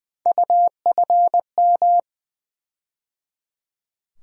Morse code, heard as a keyed tone, about 20 wpm, 700 Hz. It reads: UFM
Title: USA Call Sign Postfixes - 3 Letters 20wpm